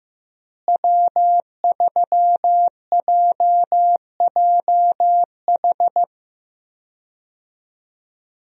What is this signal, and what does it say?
Morse code, heard as a keyed tone, about 15 wpm, 700 Hz. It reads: W3JJH
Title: USA Call Signs - 5 Characters - Repeated 2x - 15wpm